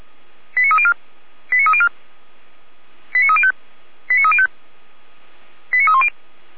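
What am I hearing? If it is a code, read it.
Five-tone selective call (ZVEI), heard as quick runs of five tones, tones of about 70 ms. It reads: 89304 89304 89305 89305 89310
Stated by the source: ZVEI five-tone call received on an SDR.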